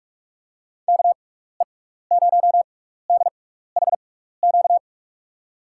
Morse code, heard as a keyed tone, about 44 wpm, spaced as for 15 wpm, 700 Hz. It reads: KE0DHQ
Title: USA Call Signs - 6x Character Spacing - 45wpm